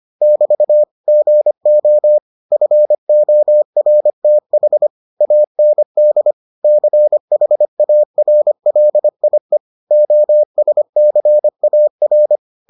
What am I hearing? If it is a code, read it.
Morse code, heard as a keyed tone, about 25 wpm, 600 Hz. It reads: <BT> GO FORTH AND CHARLIE OSCAR